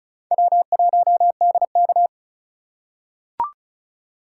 Morse code, heard as a keyed tone, about 36 wpm, 700 Hz. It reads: W1DK
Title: USA Call Signs - 3x Character Spacing - 35wpm